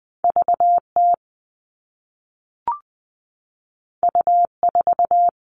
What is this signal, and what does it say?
Morse code, heard as a keyed tone, about 20 wpm, 700 Hz. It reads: VT U4
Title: All Cut Numbers - 2-Digit 20wpm